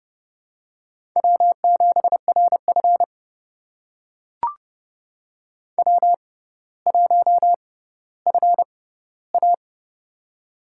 Morse code, heard as a keyed tone, about 30 wpm, spaced as for 15 wpm, 700 Hz. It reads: W7RF W1FA
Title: USA Call Signs - 4 Characters - 6x Character Spacing - 30wpm